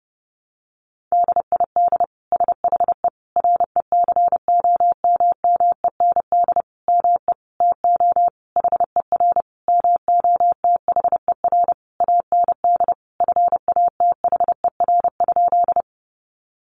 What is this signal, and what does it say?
Morse code, heard as a keyed tone, about 28 wpm, 700 Hz. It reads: DID SHE RECOMMEND ME TO HER MOTHER AND FATHER?
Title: Sentences from Top 1200 Words 30wpm